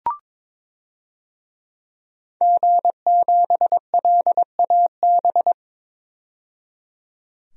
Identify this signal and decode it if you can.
Morse code, heard as a keyed tone, about 22 wpm, 700 Hz. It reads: G7LAB
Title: DXCC-100 Call Signs - 5 Characters - 22wpm